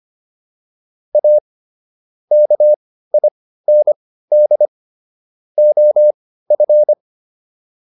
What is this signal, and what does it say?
Morse code, heard as a keyed tone, about 25 wpm, spaced as for 15 wpm, 600 Hz. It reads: A KIND OF